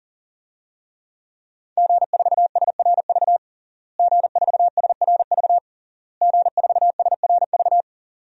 Morse code, heard as a keyed tone, about 40 wpm, 700 Hz. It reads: G4SRV G4SRV G4SRV
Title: DXCC-100 Call Signs - 5 Characters - Repeated 3x - 40wpm